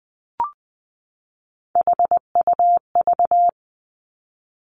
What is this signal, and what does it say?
Morse code, heard as a keyed tone, about 19 wpm, 700 Hz. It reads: HUV
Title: Three Challenging Letter-Number Combinations 20wpm